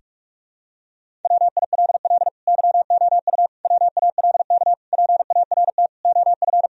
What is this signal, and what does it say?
Morse code, heard as a keyed tone, about 46 wpm, 700 Hz. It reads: WILL YOU WALK PART OF
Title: Sentences from Top 1000 Words - Review Part 1 - 45wpm